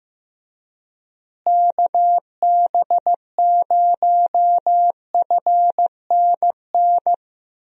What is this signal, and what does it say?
Morse code, heard as a keyed tone, about 15 wpm, 700 Hz. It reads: KB0FNN
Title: USA Call Signs - 6 Characters - Repeated 2x - 15wpm